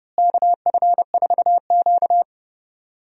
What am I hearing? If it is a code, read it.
Morse code, heard as a keyed tone, about 30 wpm, 700 Hz. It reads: KF4Q
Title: USA Call Signs - 4 Characters - Repeated 2x - 30wpm